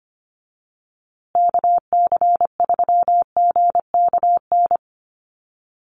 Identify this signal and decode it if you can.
Morse code, heard as a keyed tone, about 25 wpm, 700 Hz. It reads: KC3GKN
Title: USA Call Signs Repeated 2x 25wpm